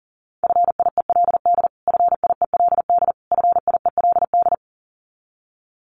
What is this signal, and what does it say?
Morse code, heard as a keyed tone, about 40 wpm, 700 Hz. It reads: FIELD FIELD FIELD
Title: Top 500 Words Repeated 40wpm